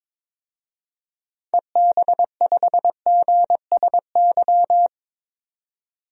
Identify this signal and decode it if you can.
Morse code, heard as a keyed tone, about 22 wpm, 700 Hz. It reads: EB5GSY
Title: DXCC-100 Call Signs - 6 Characters - 22wpm